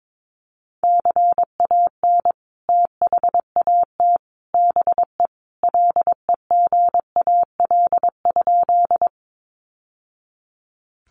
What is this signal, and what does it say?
Morse code, heard as a keyed tone, about 22 wpm, 700 Hz. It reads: CAN THAT BE LEGAL?
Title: Sentences from Top 1200 Words 22wpm